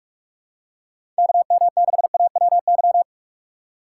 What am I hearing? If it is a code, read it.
Morse code, heard as a keyed tone, about 46 wpm, 700 Hz. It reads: KM6AWY
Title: USA Call Signs - 6x Character Spacing - 45wpm